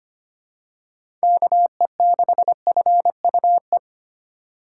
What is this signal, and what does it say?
Morse code, heard as a keyed tone, about 25 wpm, 700 Hz. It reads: KE6FUE
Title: USA Call Signs - 3x Character Spacing - 25wpm